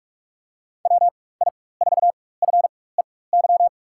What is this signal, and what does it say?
Morse code, heard as a keyed tone, about 46 wpm, spaced as for 21 wpm, 700 Hz. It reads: WI4FEY
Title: USA Call Signs - 4x Character Spacing - 45wpm